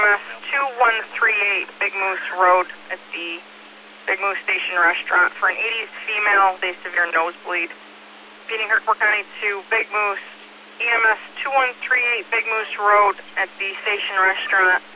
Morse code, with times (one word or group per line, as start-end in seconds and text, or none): none